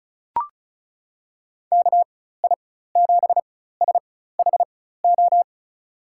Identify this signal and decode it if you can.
Morse code, heard as a keyed tone, about 34 wpm, spaced as for 16 wpm, 700 Hz. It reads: KI7SHO
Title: USA Call Signs - 6 Characters - 4x Character Spacing - 35wpm